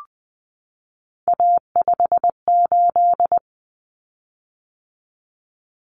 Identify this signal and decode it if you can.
Morse code, heard as a keyed tone, about 20 wpm, 700 Hz. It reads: A58
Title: Common Cut Numbers - 3-Digit 20wpm